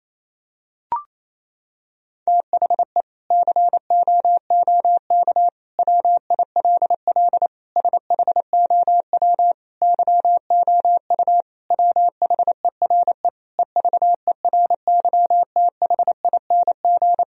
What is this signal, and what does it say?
Morse code, heard as a keyed tone, about 28 wpm, 700 Hz. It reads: THE COOK WILL SHOW YOU WHERE EVERYTHING
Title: Sentences from Top 1000 Words - Review Part 4 - 28wpm